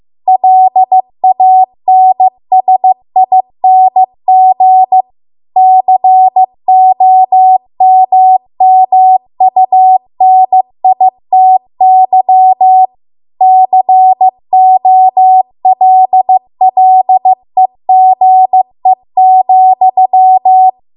Morse code, heard as a keyed tone, about 15 wpm, 750 Hz. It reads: LANSING COMMUNITY COLLEGE,
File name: Archive / 15 WPM / 15WPM